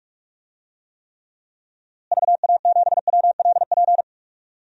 Morse code, heard as a keyed tone, about 42 wpm, 700 Hz. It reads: VA7WLP
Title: DXCC-100 Call Signs - 6 Characters - Repeated 2x - 45wpm